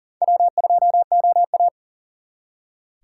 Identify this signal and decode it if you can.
Morse code, heard as a keyed tone, about 52 wpm, 700 Hz. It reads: W2OA